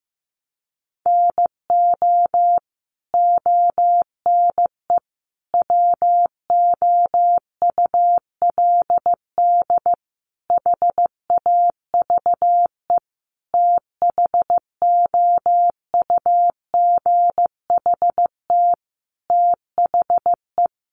Morse code, heard as a keyed tone, about 15 wpm, 700 Hz. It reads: NO ONE WOULD HAVE THOUGHT THE
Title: Sentences from Top 300 Words 15wpm